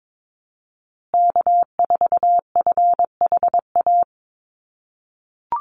Morse code, heard as a keyed tone, about 22 wpm, 700 Hz. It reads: K4FHA
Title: USA Call Signs Repeated 2x 22wpm